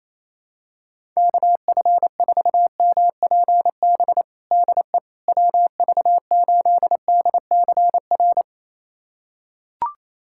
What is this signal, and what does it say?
Morse code, heard as a keyed tone, about 28 wpm, 700 Hz. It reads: KF4MPB DE WV8DCR
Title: USA Callsign DE Callsign 28wpm